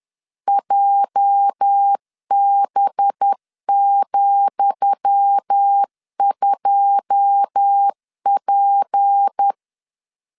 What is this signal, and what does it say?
Morse code, heard as a keyed tone, about 10 wpm, 800 Hz. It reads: JB,2P
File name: LEZIONI / 10 SIGNS / 10D COMA 10wpm 3013